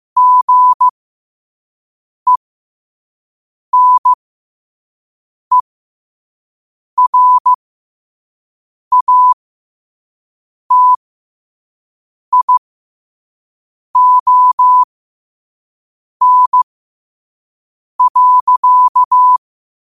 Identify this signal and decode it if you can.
Morse code, heard as a keyed tone, about 15 wpm, spaced as for 5 wpm, 1000 Hz. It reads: GENERATION.
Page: New quotes every day in morse code at 5 Words per minute.